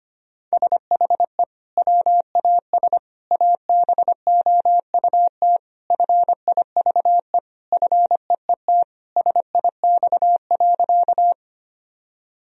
Morse code, heard as a keyed tone, about 25 wpm, 700 Hz. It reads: SHE WAS ABOUT FIVE FEET SIX.
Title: Sentences from Top 1000 Words - Review Part 2 - 25wpm